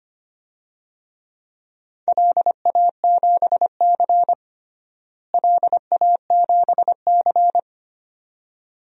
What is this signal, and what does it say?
Morse code, heard as a keyed tone, about 25 wpm, 700 Hz. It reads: LA7C LA7C
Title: DXCC-100 Call Signs - 4 Characters - Repeated 2x - 25wpm